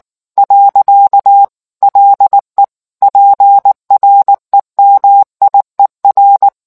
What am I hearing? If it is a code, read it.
Morse code, heard as a keyed tone, about 19 wpm, 800 Hz. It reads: . LE PREMIER